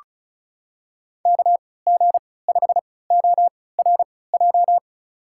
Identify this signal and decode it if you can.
Morse code, heard as a keyed tone, about 36 wpm, spaced as for 20 wpm, 700 Hz. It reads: KG5ORJ
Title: USA Call Signs - 6 Characters - 3x Character Spacing - 35wpm